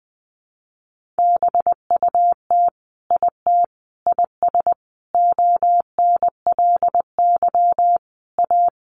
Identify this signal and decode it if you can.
Morse code, heard as a keyed tone, about 20 wpm, 700 Hz. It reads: BUT IT IS ONLY A
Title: Sentences from Top 1500 Words 20wpm